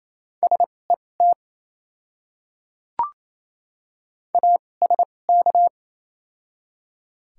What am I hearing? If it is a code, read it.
Morse code, heard as a keyed tone, about 27 wpm, spaced as for 20 wpm, 700 Hz. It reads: SET ASK
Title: 2x Character Spacing - Top 200 Words - 28wpm